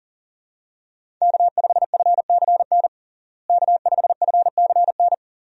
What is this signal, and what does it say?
Morse code, heard as a keyed tone, about 37 wpm, 700 Hz. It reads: K5FCN K5FCN
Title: USA Call Signs - 5 Characters - Repeated 3x - 40wpm